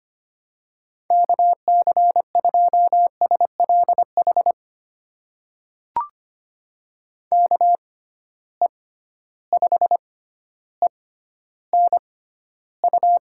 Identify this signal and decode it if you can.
Morse code, heard as a keyed tone, about 25 wpm, spaced as for 13 wpm, 700 Hz. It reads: KC2SLH KE5ENU
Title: USA Call Signs - 6 Characters - 6x Character Spacing - 25wpm